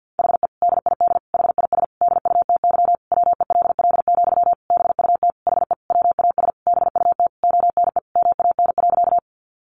Morse code, heard as a keyed tone, about 50 wpm, 700 Hz. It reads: HE DID HIS DUTY WELL, BUT HE WAS BUT ONE MAN.
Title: Sentences from Top 1500 Words 50wpm